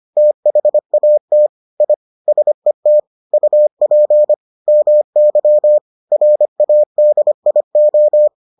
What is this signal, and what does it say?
Morse code, heard as a keyed tone, about 25 wpm, 600 Hz. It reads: THAT I SET UP MY RADIO